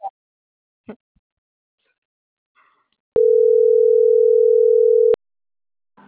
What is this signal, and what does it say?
Ringback tone (North America), 440 + 480 Hz.